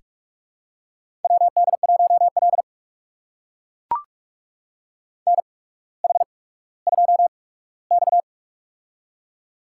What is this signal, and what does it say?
Morse code, heard as a keyed tone, about 42 wpm, spaced as for 19 wpm, 700 Hz. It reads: WD1L NH2X
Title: USA Call Signs - 4 Characters - 8x Character Spacing - 45wpm